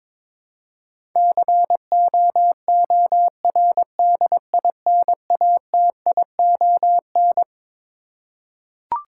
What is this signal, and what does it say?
Morse code, heard as a keyed tone, about 22 wpm, 700 Hz. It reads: COORDINATION
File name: Twelve Letter Words - 22wpm